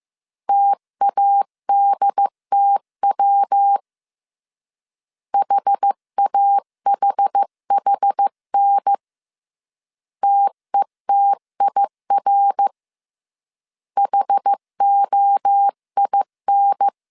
Morse code, heard as a keyed tone, about 15 wpm, 800 Hz. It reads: TADTW HAHHN TETIR HOIN
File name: CW / DET / 1 3 1500C S A 14wpm 2447